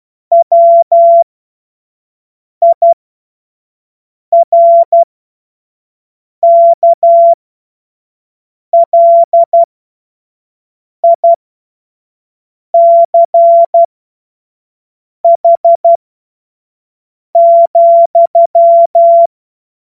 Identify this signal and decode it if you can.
Morse code, heard as a keyed tone, about 12 wpm, spaced as for 5 wpm, 680 Hz. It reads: WIRKLICH,
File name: Morsekurs / Witze / Witz 07. Juni 2022 25 F70